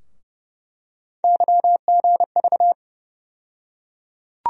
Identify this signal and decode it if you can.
Morse code, heard as a keyed tone, about 30 wpm, 700 Hz. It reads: YGV
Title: USA Call Sign Postfixes - 3 Letters 30wpm